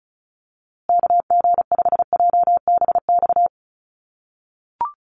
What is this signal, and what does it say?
Morse code, heard as a keyed tone, about 34 wpm, 700 Hz. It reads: KG5JBX
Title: USA Call Signs Repeated 3x 35wpm